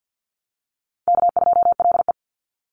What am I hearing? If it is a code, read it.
Morse code, heard as a keyed tone, about 50 wpm, 700 Hz. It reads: K2LE
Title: Easy Call Signs - 50wpm